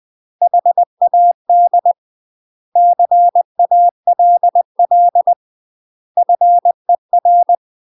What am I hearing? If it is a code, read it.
Morse code, heard as a keyed tone, about 20 wpm, 700 Hz. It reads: HAD CALL FER
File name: UK MASTER QSOs TRAINING / 20WPM / UK MASTER QSOs 0002